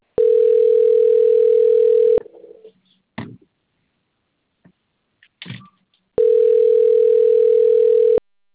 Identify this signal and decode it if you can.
Ringback tone (North America), 440 + 480 Hz.
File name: out- - - - - .34554